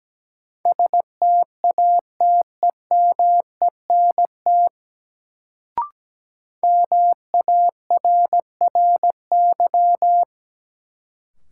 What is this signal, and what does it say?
Morse code, heard as a keyed tone, about 17 wpm, 700 Hz. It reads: STATEMENT MARRY
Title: Sentences from Top 1200 Words - New Word Drill - 17wpm